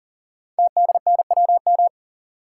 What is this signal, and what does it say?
Morse code, heard as a keyed tone, about 38 wpm, 690 Hz. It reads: TDNWM
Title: Code Groups - 5 Letters - 40wpm